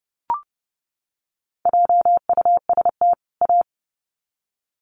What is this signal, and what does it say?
Morse code, heard as a keyed tone, about 30 wpm, 700 Hz. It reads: JUST A
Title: Sets of 2 Words 30wpm